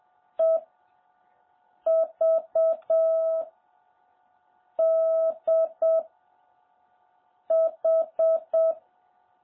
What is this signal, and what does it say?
Morse code, heard as a keyed tone, about 7 wpm, 650 Hz. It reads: E V D H
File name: jh8cbh / 2021cw